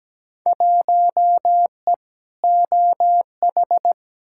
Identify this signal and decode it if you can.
Morse code, heard as a keyed tone, about 17 wpm, 700 Hz. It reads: 1E OH
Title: Field Day Exchanges - 17wpm